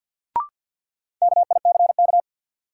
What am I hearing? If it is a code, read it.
Morse code, heard as a keyed tone, about 47 wpm, 700 Hz. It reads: KICK
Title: Sentences from Top 1300 Words - New Word Drill - 50wpm